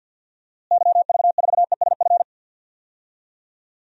Morse code, heard as a keyed tone, about 47 wpm, 700 Hz. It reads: YV4ESF
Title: DXCC-100 Call Signs - 6 Characters - Repeated 3x - 50wpm